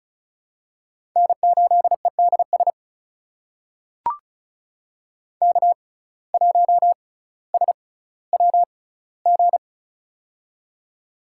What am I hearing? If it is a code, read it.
Morse code, heard as a keyed tone, about 36 wpm, spaced as for 18 wpm, 700 Hz. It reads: N8EDS K1SWG
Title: USA Call Signs - 5 Characters - 6x Character Spacing - 35wpm